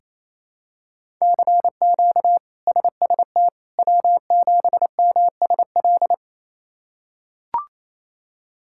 Morse code, heard as a keyed tone, about 28 wpm, 700 Hz. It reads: CQ SST W7MSL
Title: K1USN SST QSO Practice - 28wpm